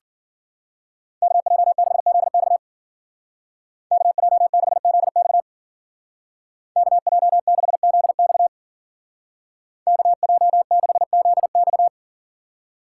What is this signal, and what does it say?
Morse code, heard as a keyed tone, about 50 wpm, 700 Hz. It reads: KJ6ZX KJ6ZX KJ6ZX KJ6ZX